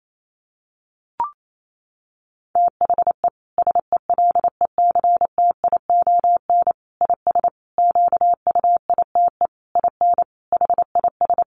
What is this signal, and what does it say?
Morse code, heard as a keyed tone, about 28 wpm, 700 Hz. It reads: THE SELECTION IS QUITE IN HIS
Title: Sentences from Top 1400 Words 28wpm